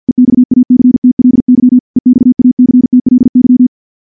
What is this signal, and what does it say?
Morse code, heard as a keyed tone, about 50 wpm, 250 Hz. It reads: EXACTLY EXACTLY
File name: morse / morse-training-50WPM